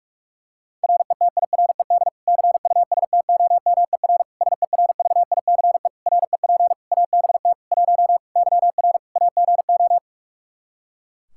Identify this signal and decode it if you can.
Morse code, heard as a keyed tone, about 42 wpm, 700 Hz. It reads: RETIRED CUSTOMER SERVICE REP ABT 1 YR AGO
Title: Practice QSO - Exchange 3 - More Info - 45wpm